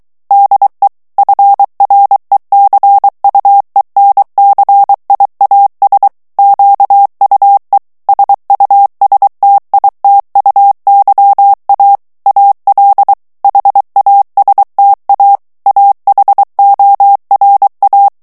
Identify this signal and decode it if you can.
Morse code, heard as a keyed tone, about 23 wpm, 800 Hz. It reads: DE FRECUENCIAS QUE SUSTITUYA AL HASTA AHORA
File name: morse mp3 / ESPAGNOL TEXTE EN CLAIR 2 VITESSE 22 MOTS MINUTES